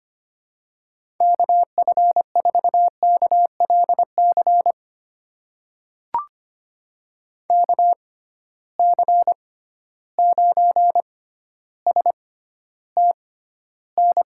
Morse code, heard as a keyed tone, about 25 wpm, spaced as for 13 wpm, 700 Hz. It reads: KF4KLC KC9STN
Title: USA Call Signs - 6 Characters - 6x Character Spacing - 25wpm